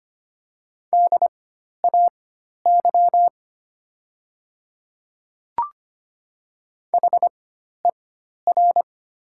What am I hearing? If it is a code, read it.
Morse code, heard as a keyed tone, about 25 wpm, spaced as for 12 wpm, 700 Hz. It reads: DAY HER
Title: 4x Character Spacing - Top 100 Words - 25wpm